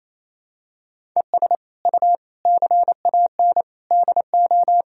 Morse code, heard as a keyed tone, about 28 wpm, 700 Hz. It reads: ES U CAN DO
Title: Sentences from Top 500 Words and Abbreviations 28wpm